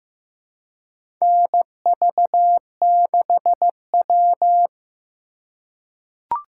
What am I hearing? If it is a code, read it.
Morse code, heard as a keyed tone, about 15 wpm, 700 Hz. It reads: NV6W
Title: USA Call Signs - 4 Characters - 3x Character Spacing - 15wpm